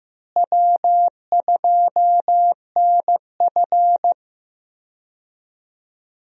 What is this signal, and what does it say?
Morse code, heard as a keyed tone, about 15 wpm, 700 Hz. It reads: W2NF
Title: USA Call Signs - 4 Characters - Repeated 3x - 15wpm